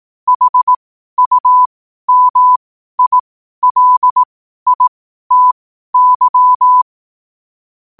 Morse code, heard as a keyed tone, about 18 wpm, spaced as for 12 wpm, 1000 Hz. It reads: HUMILITY
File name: qotdfiles / 12wpm